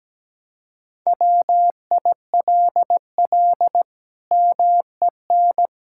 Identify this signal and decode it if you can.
Morse code, heard as a keyed tone, about 17 wpm, 700 Hz. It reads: WILL MEN
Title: Sentences from Top 1600 Words 17wpm